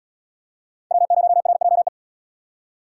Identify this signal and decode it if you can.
Morse code, heard as a keyed tone, about 66 wpm, 700 Hz. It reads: W0RJE